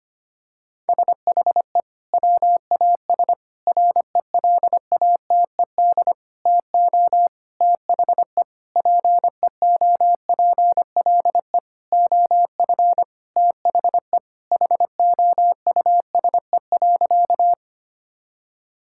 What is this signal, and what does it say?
Morse code, heard as a keyed tone, about 25 wpm, 700 Hz. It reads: SHE WAS RELATED TO THE PEOPLE OF THE HOUSE.
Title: Sentences from Top 1000 Words - Review Part 5 - 25wpm